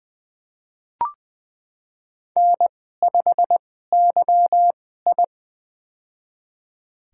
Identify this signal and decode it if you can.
Morse code, heard as a keyed tone, about 20 wpm, spaced as for 15 wpm, 700 Hz. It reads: N5YI
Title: USA Call Signs - 4 Characters - 2x Character Spacing - 20wpm